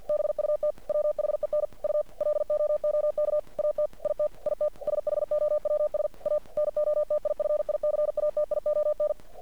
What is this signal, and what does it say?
Morse code, heard as a keyed tone, about 50 wpm, 600 Hz. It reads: BUT WHEN U LOOK AT IT IT SHOWS A NOTIFICATION
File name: archived / CRSnet